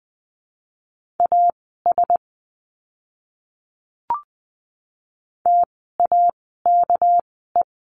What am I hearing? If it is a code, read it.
Morse code, heard as a keyed tone, about 20 wpm, spaced as for 15 wpm, 700 Hz. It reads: AS TAKE
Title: 244 Lesson 39f - Letter z - Words 20wpm